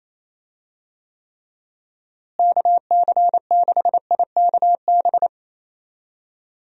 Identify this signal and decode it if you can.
Morse code, heard as a keyed tone, about 28 wpm, 700 Hz. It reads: KC6IKB
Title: USA Call Signs - 6 Characters - Repeated 3x - 28wpm